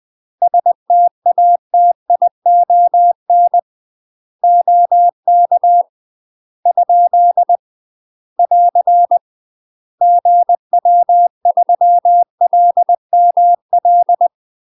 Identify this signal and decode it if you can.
Morse code, heard as a keyed tone, about 20 wpm, 700 Hz. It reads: STATION OK ? <AR> GW3LML
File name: UK MASTER QSOs TRAINING / 20WPM / UK MASTER QSOs 0011